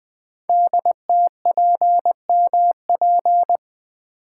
Morse code, heard as a keyed tone, about 20 wpm, 700 Hz. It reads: DTPMP
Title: Code Groups - 5 Letters - 20wpm